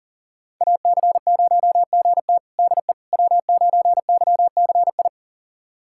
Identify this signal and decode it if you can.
Morse code, heard as a keyed tone, about 37 wpm, 700 Hz. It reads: AC0GT DE W9YCI
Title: USA Callsign DE Callsign - 5 Characters - 40wpm